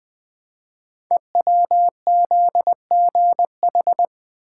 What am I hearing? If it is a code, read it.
Morse code, heard as a keyed tone, about 20 wpm, 700 Hz. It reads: EWZGH
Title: Code Groups - 5 Letters - 20wpm